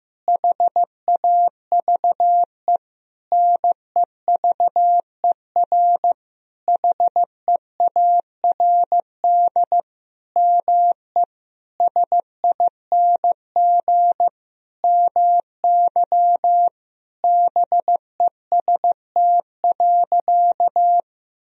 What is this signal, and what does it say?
Morse code, heard as a keyed tone, about 15 wpm, 700 Hz. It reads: HAVE NEVER HEARD ME SING MY BEST.
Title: Sentences from Top 1000 Words - Review Part 2 - 15wpm